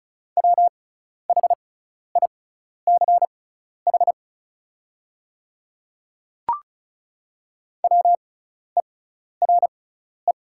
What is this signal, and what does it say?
Morse code, heard as a keyed tone, about 36 wpm, spaced as for 12 wpm, 700 Hz. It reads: WHICH WERE
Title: 6x Character Spacing - Top 100 Words - 35wpm